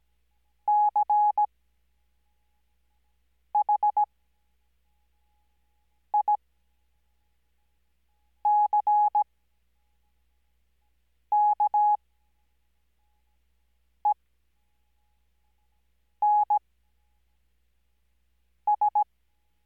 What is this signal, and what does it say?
Morse code, heard as a keyed tone, about 17 wpm, spaced as for 4 wpm, 840 Hz. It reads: CHICKENS
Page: Farnsworth Spacing: 15WPM Character speed Spaced at 5WPM.